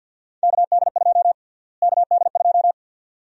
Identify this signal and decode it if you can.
Morse code, heard as a keyed tone, about 48 wpm, 700 Hz. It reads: KD2 KD2
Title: USA Call Sign Prefixes - Repeated 2x - 50wpm